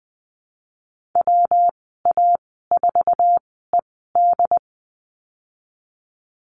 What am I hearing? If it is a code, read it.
Morse code, heard as a keyed tone, about 20 wpm, spaced as for 15 wpm, 700 Hz. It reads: WA4ED
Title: 122 Lesson 20g - Number 3 - Call Signs 20wpm